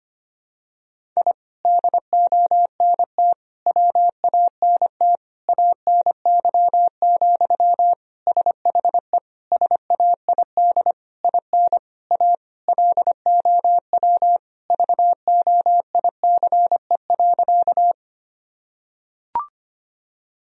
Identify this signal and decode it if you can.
Morse code, heard as a keyed tone, about 25 wpm, 700 Hz. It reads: I DONT WANT ANY, SHE SAID IN A LOW VOICE.
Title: Sentences from Top 1000 Words - Review Part 1 - 25wpm